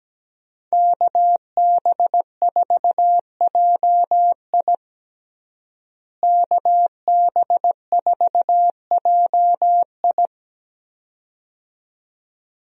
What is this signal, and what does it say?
Morse code, heard as a keyed tone, about 17 wpm, 700 Hz. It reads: KB4JI KB4JI
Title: USA Call Signs - 5 Characters - Repeated 2x - 17wpm